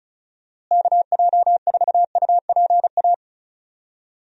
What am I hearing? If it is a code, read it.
Morse code, heard as a keyed tone, about 34 wpm, 700 Hz. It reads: KJ4UPA
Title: USA Call Signs - 6 Characters - Repeated 2x - 35wpm